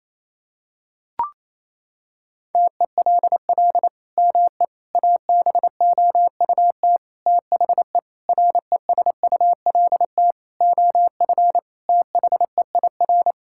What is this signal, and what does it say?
Morse code, heard as a keyed tone, about 28 wpm, 700 Hz. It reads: TELL ME ABOUT THE RESULT OF THEIR
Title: Sentences from Top 1000 Words - Review Part 5 - 28wpm